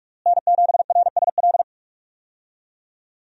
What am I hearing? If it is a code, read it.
Morse code, heard as a keyed tone, about 41 wpm, 700 Hz. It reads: N7RSL
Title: USA Call Signs - 5 Characters - Repeated 3x - 45wpm